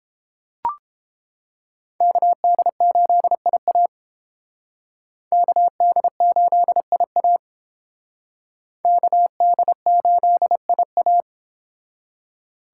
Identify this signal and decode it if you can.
Morse code, heard as a keyed tone, about 30 wpm, 700 Hz. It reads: KD8IA KD8IA KD8IA